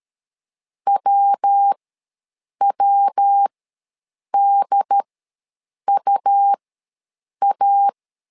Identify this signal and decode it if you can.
Morse code, heard as a keyed tone, about 13 wpm, spaced as for 7 wpm, 800 Hz. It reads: WWDUA
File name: LEZIONI / 3 MWRO / 3 MWRO ws 12wpm 1914